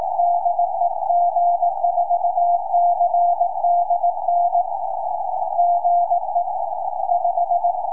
Morse code, heard as a keyed tone, about 18 wpm, 700 Hz, with radio noise. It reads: DE G4CDN GE 5
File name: tropo / g4cdn 2m